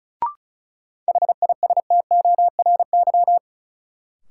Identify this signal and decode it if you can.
Morse code, heard as a keyed tone, about 36 wpm, 700 Hz. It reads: HISTORY
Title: Sets of 3 Words - New Word Drill - 35wpm